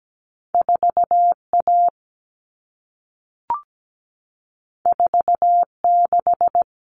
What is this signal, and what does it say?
Morse code, heard as a keyed tone, about 17 wpm, 700 Hz. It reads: 4A 46
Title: All Cut Numbers - 2-Digit 17wpm